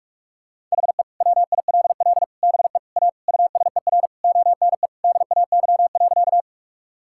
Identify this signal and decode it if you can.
Morse code, heard as a keyed tone, about 42 wpm, 700 Hz. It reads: HE WILL BE A USER ONE DAY.
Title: Sentences from Top 1000 Words - Review Part 5 - 45wpm